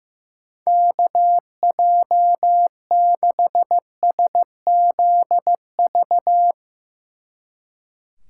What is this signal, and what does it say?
Morse code, heard as a keyed tone, about 15 wpm, 700 Hz. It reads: KJ6SZV
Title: Speed Racing - USA 2x3 Call Signs 15wpm